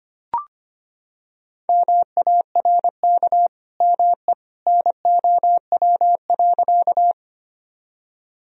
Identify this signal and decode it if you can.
Morse code, heard as a keyed tone, about 25 wpm, 700 Hz. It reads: MARK ME NOW.
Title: Sentences from Top 1000 Words - Review Part 1 - 25wpm